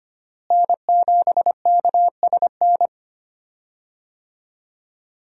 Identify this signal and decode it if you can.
Morse code, heard as a keyed tone, about 25 wpm, 700 Hz. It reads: N7KSN
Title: USA Call Signs - 5 Characters - Repeated 2x - 25wpm